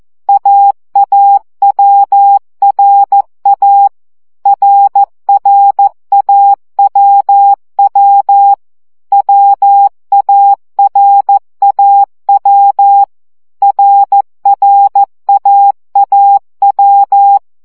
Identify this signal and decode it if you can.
Morse code, heard as a keyed tone, about 14 wpm, 800 Hz. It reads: AAWRA RRAWW WARAW RRAAW
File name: CW gyakorlas a r w 12 baud